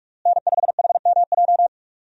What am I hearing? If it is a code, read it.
Morse code, heard as a keyed tone, about 42 wpm, 700 Hz. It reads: N5HMJ
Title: USA Call Signs - 5 Characters - Repeated 3x - 45wpm